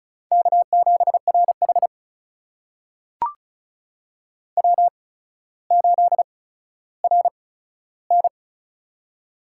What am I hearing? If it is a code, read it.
Morse code, heard as a keyed tone, about 34 wpm, spaced as for 15 wpm, 700 Hz. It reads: K7RH W8RN
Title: USA Call Signs - 4 Characters - 8x Character Spacing - 35wpm